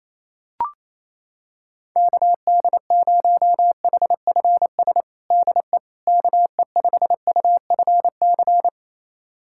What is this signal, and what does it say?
Morse code, heard as a keyed tone, about 28 wpm, 700 Hz. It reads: KD0HFS DE KE5UFC
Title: USA Callsign DE Callsign - 6 Characters - 28wpm